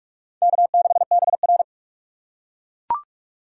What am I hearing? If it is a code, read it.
Morse code, heard as a keyed tone, about 40 wpm, 700 Hz. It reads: K6BR